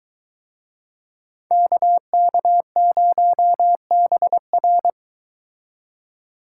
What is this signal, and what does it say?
Morse code, heard as a keyed tone, about 24 wpm, 700 Hz. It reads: KK0BR